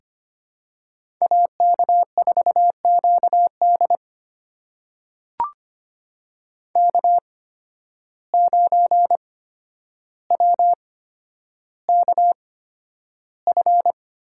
Morse code, heard as a keyed tone, about 25 wpm, spaced as for 11 wpm, 700 Hz. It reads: AK4QD K9WKF
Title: USA Call Signs - 5 Characters - 8x Character Spacing - 25wpm